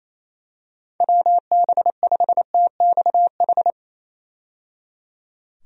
Morse code, heard as a keyed tone, about 28 wpm, 700 Hz. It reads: WB5TXH
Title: USA Call Signs - 6 Characters - 28wpm